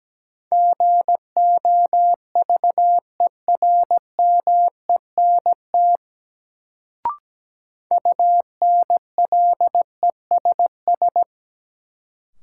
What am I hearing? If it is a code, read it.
Morse code, heard as a keyed tone, about 17 wpm, 700 Hz. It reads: GOVERMENT UNLESS
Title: Sentences from Top 1300 Words - New Word Drill Alt - 17wpm